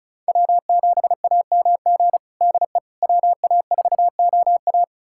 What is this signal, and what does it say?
Morse code, heard as a keyed tone, about 36 wpm, 700 Hz. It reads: W7AMG DE WA4OA
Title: USA Callsign DE Callsign - 5 Characters - 35wpm